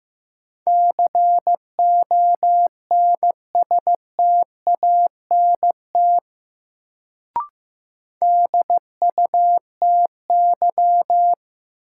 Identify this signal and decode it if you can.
Morse code, heard as a keyed tone, about 15 wpm, 700 Hz. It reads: CONSTANT DUTY
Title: Sentences from Top 1500 Words - New Word Drill Alt - 15wpm